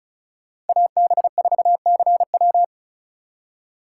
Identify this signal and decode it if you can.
Morse code, heard as a keyed tone, about 36 wpm, 690 Hz. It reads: AB4CW
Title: USA Call Signs - 5 Characters - 3x Character Spacing - 35wpm